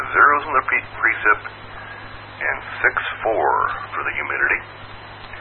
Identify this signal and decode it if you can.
Emergency Alert System (SAME) tones, heard as harsh buzzing bursts, with speech. It is heard from 0.7 to 4.7 s.